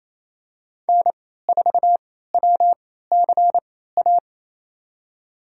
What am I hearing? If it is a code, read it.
Morse code, heard as a keyed tone, about 28 wpm, spaced as for 16 wpm, 700 Hz. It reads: N4WCA
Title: USA Call Signs - 3x Character Spacing - 28wpm